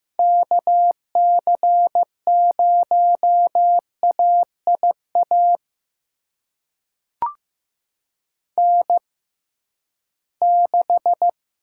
Morse code, heard as a keyed tone, about 15 wpm, spaced as for 11 wpm, 700 Hz. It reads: KC0AIA N6
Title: USA Call Signs - 6x Character Spacing - 15wpm